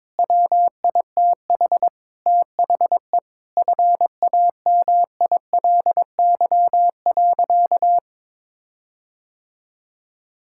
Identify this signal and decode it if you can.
Morse code, heard as a keyed tone, about 22 wpm, 700 Hz. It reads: WITH THE FAMILY.
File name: Sentences with 7-Word Phrases - 22wpm